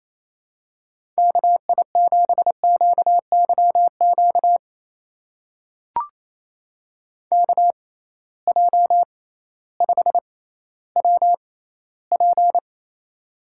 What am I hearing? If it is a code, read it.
Morse code, heard as a keyed tone, about 28 wpm, spaced as for 15 wpm, 700 Hz. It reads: KI7QYQ KJ5WP
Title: USA Call Signs - 6x Character Spacing - 28wpm